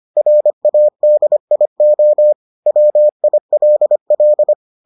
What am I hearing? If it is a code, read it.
Morse code, heard as a keyed tone, about 25 wpm, 600 Hz. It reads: RADIO WILL